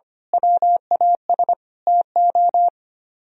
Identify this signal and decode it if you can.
Morse code, heard as a keyed tone, about 28 wpm, 700 Hz. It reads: WAS TO
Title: Speed-Racing - Sets of 5 Words 25wpm